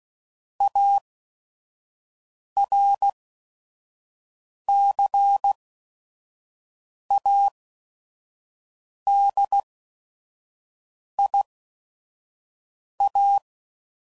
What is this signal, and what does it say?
Morse code, heard as a keyed tone, about 16 wpm, spaced as for 5 wpm, 800 Hz. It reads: ARCADIA
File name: CW Code Groups @5WPM / fromSM's / Cities